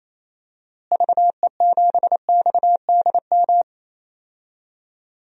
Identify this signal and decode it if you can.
Morse code, heard as a keyed tone, about 28 wpm, 700 Hz. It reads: VE7XDM
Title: DXCC-100 Call Signs - 6 Characters - Repeated 3x - 28wpm